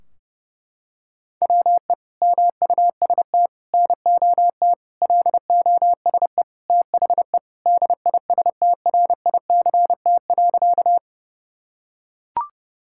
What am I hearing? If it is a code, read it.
Morse code, heard as a keyed tone, about 32 wpm, 700 Hz. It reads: WE MUST NOT LOSE THE DISTRICT.
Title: Sentences from Top 1600 Words 30wpm